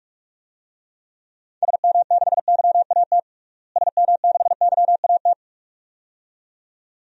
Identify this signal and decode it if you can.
Morse code, heard as a keyed tone, about 46 wpm, 700 Hz. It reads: SM6YAT SM6YAT
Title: DXCC-100 Call Signs - 6 Characters - Repeated 2x - 45wpm